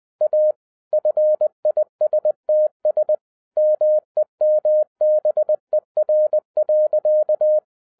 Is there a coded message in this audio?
Yes: Morse code from 0.2 to 7.6 s, heard as a keyed tone, about 20 wpm, 610 Hz. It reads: A FISTS MEMBER.